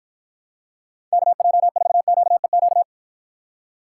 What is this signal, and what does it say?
Morse code, heard as a keyed tone, about 53 wpm, 700 Hz. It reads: KJ4QEQ